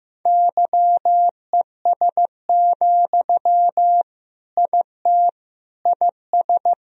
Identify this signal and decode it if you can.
Morse code, heard as a keyed tone, about 15 wpm, 700 Hz. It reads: YES, IT IS
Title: Sentences from Top 1000 Words - Review Part 1 - 15wpm